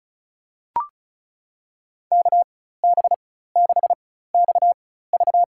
Morse code, heard as a keyed tone, about 36 wpm, spaced as for 16 wpm, 700 Hz. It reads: KB6XV
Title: USA Call Signs - 5 Characters - 4x Character Spacing - 35wpm